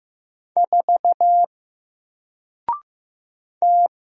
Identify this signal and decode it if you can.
Morse code, heard as a keyed tone, about 15 wpm, 700 Hz. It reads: 4 T
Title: Single Letter-Number 15wpm